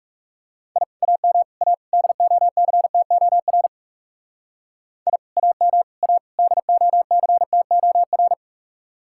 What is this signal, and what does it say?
Morse code, heard as a keyed tone, about 38 wpm, 700 Hz. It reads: I AM A DOCTOR I AM A DOCTOR